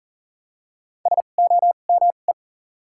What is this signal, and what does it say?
Morse code, heard as a keyed tone, about 39 wpm, spaced as for 29 wpm, 700 Hz. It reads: SOME
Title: 2x Character Spacing - Top 200 Words - 40wpm